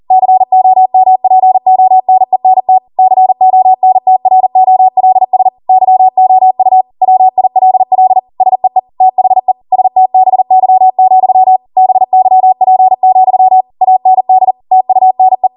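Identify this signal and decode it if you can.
Morse code, heard as a keyed tone, about 40 wpm, 750 Hz. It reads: COMPONENT CONTROLS YOU WILL SEE THE STBY, BYP, AND TUNE
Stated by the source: Code practice at 40 wpm.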